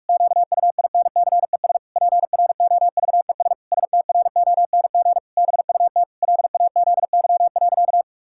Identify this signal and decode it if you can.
Morse code, heard as a keyed tone, about 46 wpm, 700 Hz. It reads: QUINCES PROVES STRONG BUT LAZY.
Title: Pangram Challenge v2 - 45wpm